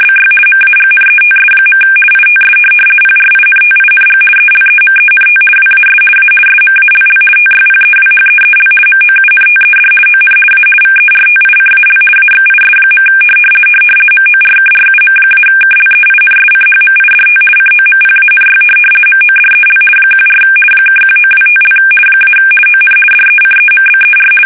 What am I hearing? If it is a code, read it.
RTTY, about 50 baud. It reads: BY MAGNETS. ITS PRINTING ELEMENT WAS A SMALL TYPEWHEEL MOUNTED ON A VERTICAL AXIS. MORKRUM-KLEINSCHMIDT COMPANY MODEL 14 (1925) - ON DECEMBER
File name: fsk850-50B / teletype-corp